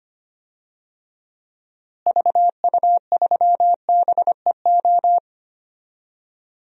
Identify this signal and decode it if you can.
Morse code, heard as a keyed tone, about 25 wpm, 700 Hz. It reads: VU3BEO
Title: DXCC-100 Call Signs - 6 Characters - Repeated 2x - 25wpm